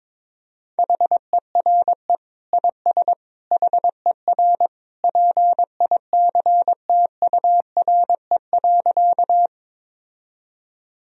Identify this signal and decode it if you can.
Morse code, heard as a keyed tone, about 22 wpm, 700 Hz. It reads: HERE IS HER PICTURE.
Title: Sentences from Top 1000 Words - Review Part 1 - 22wpm